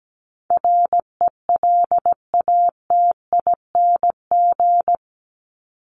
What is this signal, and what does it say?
Morse code, heard as a keyed tone, about 17 wpm, 700 Hz. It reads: RELATING
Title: ING Suffix - Top 100 Words with Base Words 17wpm